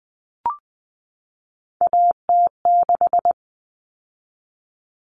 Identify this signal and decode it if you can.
Morse code, heard as a keyed tone, about 20 wpm, 700 Hz. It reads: AT6
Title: Common Cut Numbers - 3-Digit 20wpm